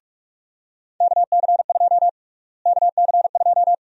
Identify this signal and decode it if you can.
Morse code, heard as a keyed tone, about 43 wpm, 700 Hz. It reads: KC2 KC2
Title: USA Call Sign Prefixes - Repeated 2x - 45wpm